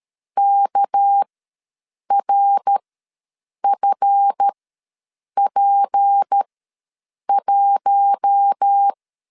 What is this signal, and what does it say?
Morse code, heard as a keyed tone, about 13 wpm, spaced as for 7 wpm, 800 Hz. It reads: KRFP1
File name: LEZIONI / 7 GF9Z / 7 GF ws 12wpm 3140